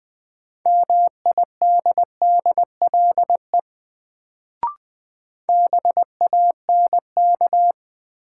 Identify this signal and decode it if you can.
Morse code, heard as a keyed tone, about 20 wpm, 700 Hz. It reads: MIDDLE BANK
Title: Sentences from Top 900 Words - New Word Drill - 20wpm